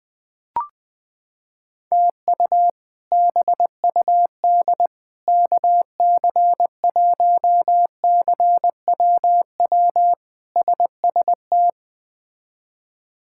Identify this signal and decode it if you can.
Morse code, heard as a keyed tone, about 20 wpm, 700 Hz. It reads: TU BUD KC1CWW SST
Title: K1USN SST QSO Practice - 20wpm